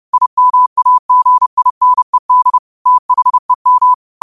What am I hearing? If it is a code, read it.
Morse code, heard as a keyed tone, about 30 wpm, 1000 Hz. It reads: IMAGINED THEM
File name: qotdfiles / 30wpm